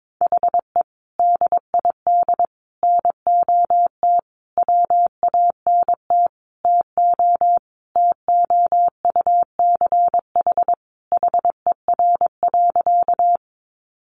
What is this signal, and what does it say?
Morse code, heard as a keyed tone, about 22 wpm, 700 Hz. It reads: HE DID NOT WANT TO TOUCH HER.
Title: Sentences from Top 1200 Words 22wpm